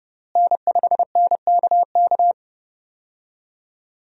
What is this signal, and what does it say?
Morse code, heard as a keyed tone, about 32 wpm, 700 Hz. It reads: N5NKK
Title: USA Call Signs - 5 Characters - Repeated 2x - 30wpm